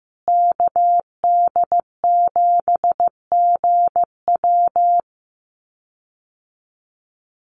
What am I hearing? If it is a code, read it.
Morse code, heard as a keyed tone, about 15 wpm, 700 Hz. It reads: KD7GW
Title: USA Call Signs Repeated 2x 15wpm